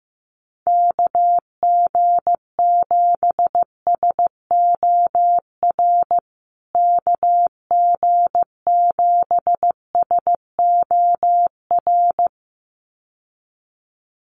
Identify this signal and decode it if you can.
Morse code, heard as a keyed tone, about 15 wpm, 700 Hz. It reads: KG7SOR KG7SOR
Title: USA Call Signs Repeated 2x 15wpm